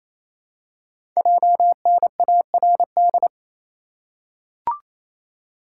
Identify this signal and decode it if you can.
Morse code, heard as a keyed tone, about 27 wpm, 700 Hz. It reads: JNARD
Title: Code Groups - 5 Letters - 28wpm